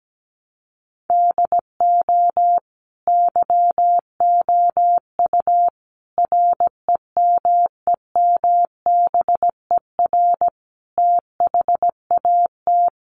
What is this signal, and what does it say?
Morse code, heard as a keyed tone, about 17 wpm, 700 Hz. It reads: DO YOU REMEMBER THAT
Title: Sentences from Top 1100 Words 17wpm